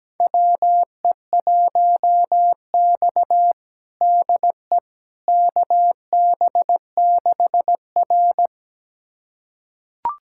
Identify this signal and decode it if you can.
Morse code, heard as a keyed tone, about 17 wpm, 700 Hz. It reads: WE1X DE KB6R
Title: USA Callsign DE Callsign - 4 Characters - 17wpm